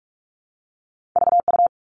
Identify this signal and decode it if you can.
Morse code, heard as a keyed tone, about 46 wpm, 700 Hz. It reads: VU
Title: All Cut Numbers - 2-Digit 45wpm